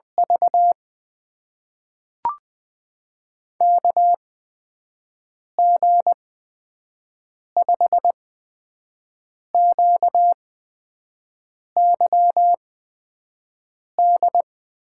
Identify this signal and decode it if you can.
Morse code, heard as a keyed tone, about 20 wpm, spaced as for 5 wpm, 700 Hz. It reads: V KG5QYD
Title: USA Call Signs - 6 Characters - 8x Character Spacing - 20wpm